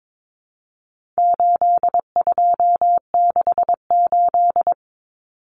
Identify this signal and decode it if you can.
Morse code, heard as a keyed tone, about 22 wpm, 700 Hz. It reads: 8268
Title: Numbers - 4-Digit 22wpm